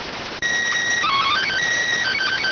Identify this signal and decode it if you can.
SSTV (slow-scan television), PD 290.